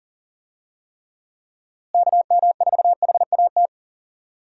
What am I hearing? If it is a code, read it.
Morse code, heard as a keyed tone, about 40 wpm, 700 Hz. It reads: KM4HAT
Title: USA Call Signs - 6 Characters - Repeated 2x - 40wpm